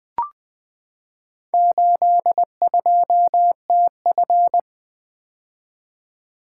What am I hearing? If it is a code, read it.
Morse code, heard as a keyed tone, about 20 wpm, 700 Hz. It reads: 82TF
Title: Four Letter-Number Combinations - 20wpm